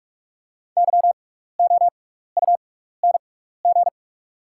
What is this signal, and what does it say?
Morse code, heard as a keyed tone, about 44 wpm, spaced as for 15 wpm, 690 Hz. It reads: YOUNG
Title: 6x Character Spacing - Top 500 Words - 45wpm